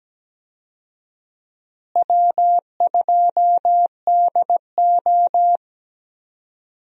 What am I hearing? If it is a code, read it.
Morse code, heard as a keyed tone, about 17 wpm, 700 Hz. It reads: W2DO
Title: USA Call Signs - 4 Characters - Repeated 3x - 17wpm